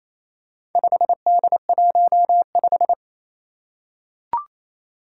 Morse code, heard as a keyed tone, about 27 wpm, 700 Hz. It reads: 5D15
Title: Four Challenging Letter-Number Combinations 28wpm